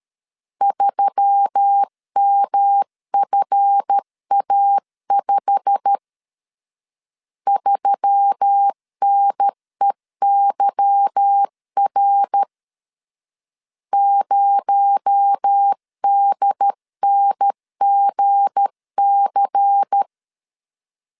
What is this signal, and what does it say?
Morse code, heard as a keyed tone, about 13 wpm, 800 Hz. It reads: 3MFA5 3NEYR 0DNGC
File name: LEZIONI / 8 YJ45 / 8 YJ45 12wpm 2151